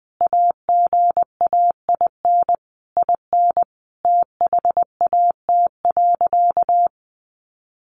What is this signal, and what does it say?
Morse code, heard as a keyed tone, about 20 wpm, 700 Hz. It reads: AGAIN IN THAT.
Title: Sentences from Top 200 Words 20wpm